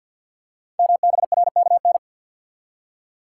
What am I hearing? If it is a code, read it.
Morse code, heard as a keyed tone, about 51 wpm, 700 Hz. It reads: MBRKN
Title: Code Groups - 5 Letters - 50wpm